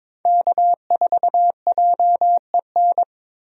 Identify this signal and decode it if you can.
Morse code, heard as a keyed tone, about 21 wpm, 700 Hz. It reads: K4JEN
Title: USA Call Signs - 5 Characters - Repeated 3x - 22wpm